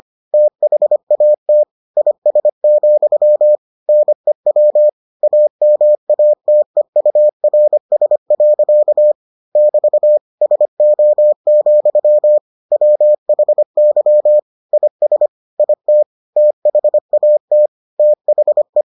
Morse code, heard as a keyed tone, about 25 wpm, 600 Hz. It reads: THAT IS, NEW AMATEURS. <BT> SO, WHY IS IT THAT THE